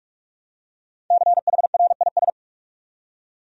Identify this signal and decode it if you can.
Morse code, heard as a keyed tone, about 46 wpm, 700 Hz. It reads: CHRIS
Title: USA Common Names - 45wpm